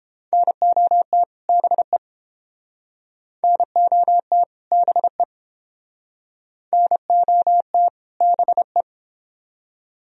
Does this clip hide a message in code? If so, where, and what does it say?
Yes: Morse code from 0.3 to 8.8 s, heard as a keyed tone, about 31 wpm, 700 Hz. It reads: NOT BE NOT BE NOT BE